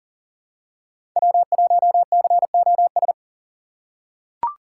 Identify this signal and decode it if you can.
Morse code, heard as a keyed tone, about 40 wpm, 700 Hz. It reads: W1COS
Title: USA Call Signs - 5 Characters - 4x Character Spacing - 40wpm